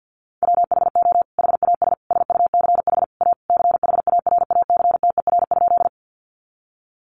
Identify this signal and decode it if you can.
Morse code, heard as a keyed tone, about 50 wpm, 700 Hz. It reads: WHO HAS SUCH A CHARACTER?
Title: Sentences from Top 1200 Words 50wpm